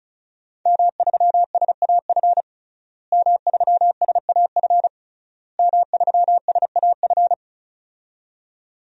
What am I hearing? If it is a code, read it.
Morse code, heard as a keyed tone, about 36 wpm, 700 Hz. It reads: M3SAF M3SAF M3SAF
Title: DXCC-100 Call Signs - 5 Characters - Repeated 3x - 35wpm